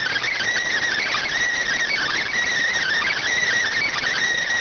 Emergency Alert System (SAME) tones, heard as harsh buzzing bursts. It is heard from 0.9 to 3.5 s.